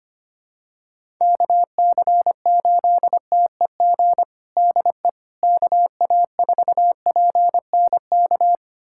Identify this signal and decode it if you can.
Morse code, heard as a keyed tone, about 25 wpm, 700 Hz. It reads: KC8TEG DE KA4PNK
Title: USA Callsign DE Callsign 25wpm